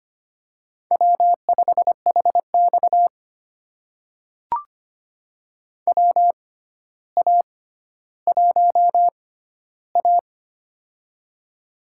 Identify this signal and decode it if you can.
Morse code, heard as a keyed tone, about 25 wpm, spaced as for 13 wpm, 700 Hz. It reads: W5HX WA1A
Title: USA Call Signs - 4 Characters - 6x Character Spacing - 25wpm